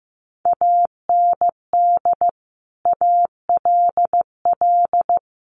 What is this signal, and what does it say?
Morse code, heard as a keyed tone, about 15 wpm, 700 Hz. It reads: AND ALL
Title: Sets of 5 Words 15wpm